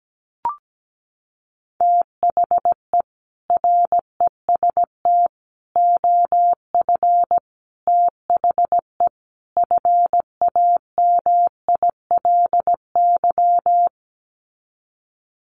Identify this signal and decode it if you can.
Morse code, heard as a keyed tone, about 17 wpm, 700 Hz. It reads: THE REST OF THE FAMILY
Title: Sets of 5 Words Encore2 17wpm